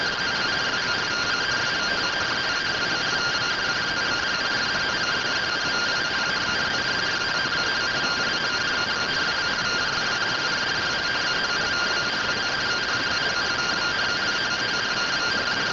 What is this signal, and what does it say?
Two-tone teleprinter signal, about 170 Hz shift.
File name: findings / MR CORONAL fr1 -131.gen-35 rtty rx